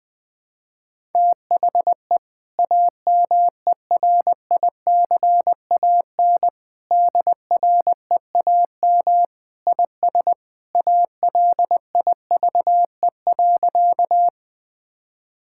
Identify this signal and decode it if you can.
Morse code, heard as a keyed tone, about 20 wpm, 700 Hz. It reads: THE AMERICAN DREAM IS ALIVE.
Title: Sentences with 2-Word Phrases - Spoken Beginning - 20wpm